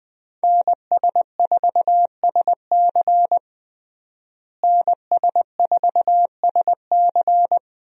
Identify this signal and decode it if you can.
Morse code, heard as a keyed tone, about 21 wpm, 700 Hz. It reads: NS4SC NS4SC
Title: USA Call Signs - 5 Characters - Repeated 2x - 20wpm